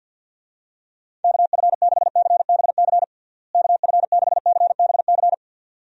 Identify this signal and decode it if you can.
Morse code, heard as a keyed tone, about 47 wpm, 700 Hz. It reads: KF6CBC KF6CBC
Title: USA Call Signs - 6 Characters - Repeated 2x - 50wpm